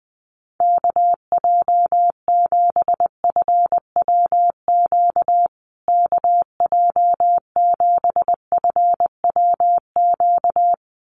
Morse code, heard as a keyed tone, about 19 wpm, 700 Hz. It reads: KJ7FWQ KJ7FWQ
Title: USA Call Signs Repeated 2x 20wpm